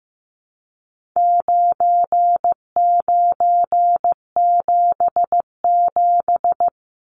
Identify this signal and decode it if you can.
Morse code, heard as a keyed tone, about 15 wpm, 700 Hz. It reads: 9977
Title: Numbers - 4-Digit 15wpm